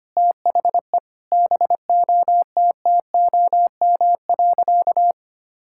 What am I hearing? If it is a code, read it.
Morse code, heard as a keyed tone, about 25 wpm, 700 Hz. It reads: THE BOTTOM.
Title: Sentences from Top 1000 Words - Review Part 6 - 25wpm